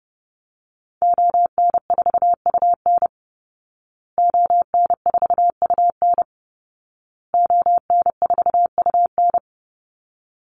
Morse code, heard as a keyed tone, about 28 wpm, 700 Hz. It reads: ON4UN ON4UN ON4UN
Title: Call Signs Repeated 2x - 30wpm